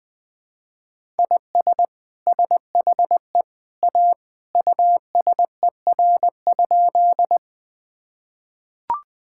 Morse code, heard as a keyed tone, about 20 wpm, 700 Hz. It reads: IS SHE A USER?
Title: Sentences from Top 1000 Words - Review Part 5 - 20wpm